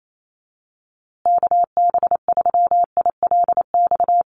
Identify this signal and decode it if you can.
Morse code, heard as a keyed tone, about 28 wpm, 700 Hz. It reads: KB3ILX
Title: USA Call Signs Repeated 3x 28wpm